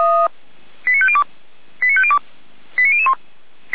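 Five-tone selective call (ZVEI), heard as quick runs of five tones, tones of about 70 ms. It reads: 89502 89502 89001
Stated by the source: Received by SDR as a ZVEI five-tone selective call.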